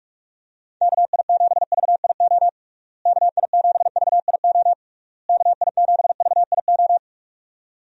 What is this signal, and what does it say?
Morse code, heard as a keyed tone, about 42 wpm, 700 Hz. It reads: KI7VIO KI7VIO KI7VIO
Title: USA Call Signs - 6 Characters - Repeated 3x - 45wpm